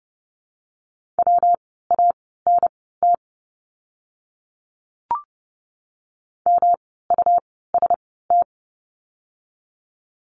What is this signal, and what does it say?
Morse code, heard as a keyed tone, about 30 wpm, spaced as for 17 wpm, 700 Hz. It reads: WANT MUST
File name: course / 30:10wpm / 237 Lesson 38f - Letter q - Words 30wpm